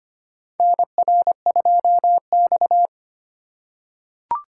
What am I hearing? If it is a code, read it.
Morse code, heard as a keyed tone, about 25 wpm, 700 Hz. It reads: NR2X
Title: USA Call Signs - 4 Characters - 2x Character Spacing - 25wpm